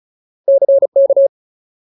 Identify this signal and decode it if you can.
Morse code, heard as a keyed tone, about 34 wpm, 550 Hz. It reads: CK
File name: Ultimate Word List / 35WPM Ultimate